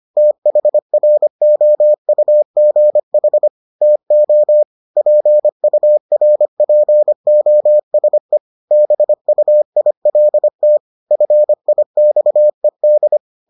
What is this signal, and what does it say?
Morse code, heard as a keyed tone, about 25 wpm, 600 Hz. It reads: THROUGH TO PURPOSE BUILT FIXED